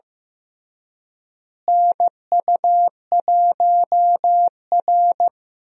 Morse code, heard as a keyed tone, about 15 wpm, 700 Hz. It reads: NU1R
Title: USA Call Signs - 4 Characters - Repeated 3x - 15wpm